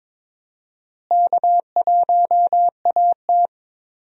Morse code, heard as a keyed tone, about 23 wpm, 700 Hz. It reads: K1AT
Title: USA Call Signs - 4 Characters - Repeated 3x - 22wpm